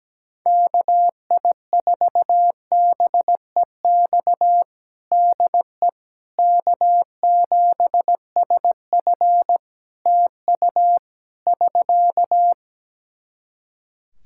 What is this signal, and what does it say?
Morse code, heard as a keyed tone, about 17 wpm, 700 Hz. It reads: KI4BEX DE K7SF TU <SK>
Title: Practice QSO - Exchange 4 - Ending - 17wpm